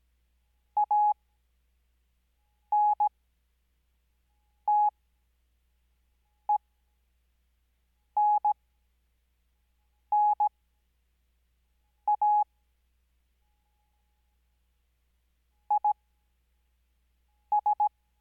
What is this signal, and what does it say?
Morse code, heard as a keyed tone, about 17 wpm, spaced as for 5 wpm, 840 Hz. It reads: ANTENNA IS
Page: Farnsworth Spacing: 15WPM Character speed Spaced at 5WPM.